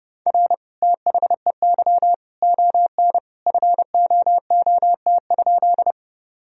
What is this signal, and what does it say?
Morse code, heard as a keyed tone, about 32 wpm, 700 Hz. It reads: R THEY ON FOOT?
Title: Sentences from Top 500 Words and Abbreviations 30wpm